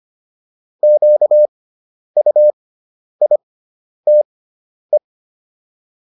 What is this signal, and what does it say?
Morse code, heard as a keyed tone, about 25 wpm, spaced as for 10 wpm, 600 Hz. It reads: QUITE